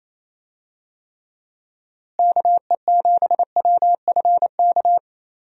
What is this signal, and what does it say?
Morse code, heard as a keyed tone, about 28 wpm, 700 Hz. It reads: KE7WFK
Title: USA Call Signs - 6 Characters - Repeated 2x - 28wpm